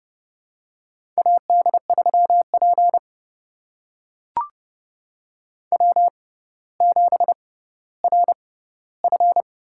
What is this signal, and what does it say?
Morse code, heard as a keyed tone, about 30 wpm, spaced as for 15 wpm, 700 Hz. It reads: AD3P W7RF
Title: USA Call Signs - 4 Characters - 6x Character Spacing - 30wpm